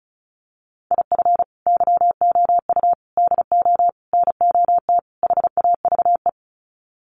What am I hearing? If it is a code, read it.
Morse code, heard as a keyed tone, about 34 wpm, 700 Hz. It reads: IF YOU DO NOT HAVE
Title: Sets of 5 Words 35wpm